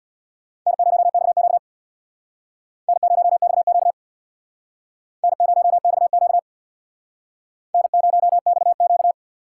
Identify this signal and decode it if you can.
Morse code, heard as a keyed tone, about 62 wpm, 700 Hz. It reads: N0XQ N0XQ N0XQ N0XQ
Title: Speed Racing - USA Call Signs 50wpm